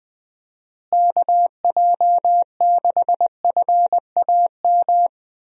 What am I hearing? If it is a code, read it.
Morse code, heard as a keyed tone, about 21 wpm, 700 Hz. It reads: KJ6FAM
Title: USA Call Signs - 6 Characters - Repeated 2x - 20wpm